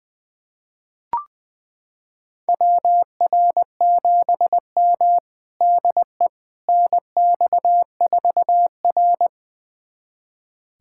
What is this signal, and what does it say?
Morse code, heard as a keyed tone, about 20 wpm, 700 Hz. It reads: WR7M DE NX4R
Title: USA Callsign DE Callsign - 4 Characters - 20wpm